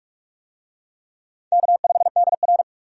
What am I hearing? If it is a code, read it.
Morse code, heard as a keyed tone, about 46 wpm, 700 Hz. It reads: K5DR
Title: USA Call Signs - 4 Characters - 45wpm